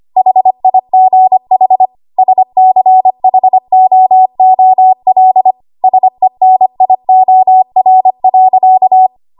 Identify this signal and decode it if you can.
Morse code, heard as a keyed tone, about 25 wpm, 750 Hz. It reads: HIGH SCHOOL SENIOR.
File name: Morse / 25WPM